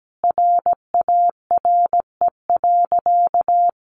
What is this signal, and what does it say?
Morse code, heard as a keyed tone, about 17 wpm, 700 Hz. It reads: RARE.
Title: Sentences from Top 1100 Words 17wpm